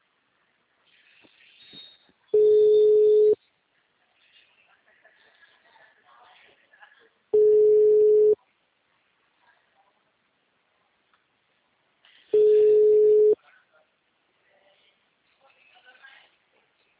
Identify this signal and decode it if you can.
Ringback tone (Europe, 425 Hz), with speech.
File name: out- -2029- - - .92648